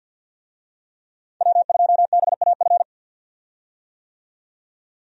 Morse code, heard as a keyed tone, about 51 wpm, 700 Hz. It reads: W2BAF
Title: USA Call Signs - 5 Characters - 50wpm